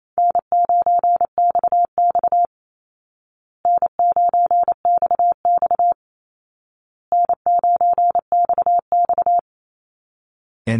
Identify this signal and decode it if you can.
Morse code, heard as a keyed tone, about 28 wpm, 700 Hz. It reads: N9XX N9XX N9XX
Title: Call Signs Repeated 2x - 28wpm